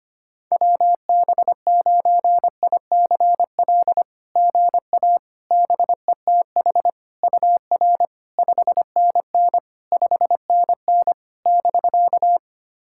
Morse code, heard as a keyed tone, about 25 wpm, 690 Hz. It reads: WB9ICL GA BETH UR 5NN 5NN <BK>
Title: SOTA QSO Practice 25wpm